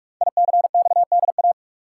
Morse code, heard as a keyed tone, about 44 wpm, 700 Hz. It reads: ICXDA
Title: Code Groups - 5 Letter:Numbers - 45wpm